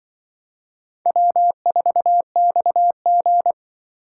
Morse code, heard as a keyed tone, about 25 wpm, 700 Hz. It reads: W4XG
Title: Speed Racing - USA 1x2 Call Signs 20wpm